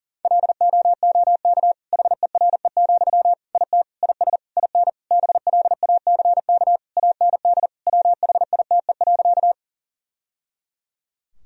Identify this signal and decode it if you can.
Morse code, heard as a keyed tone, about 40 wpm, 700 Hz. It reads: LOOK HERE, IT IS IN BLACK AND WHITE.
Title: Sentences from Top 1000 Words - Review Part 2 - 40wpm